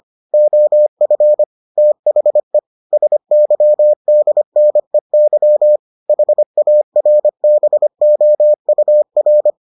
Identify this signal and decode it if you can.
Morse code, heard as a keyed tone, about 25 wpm, 600 Hz. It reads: OF THE SYDNEY HARBOUR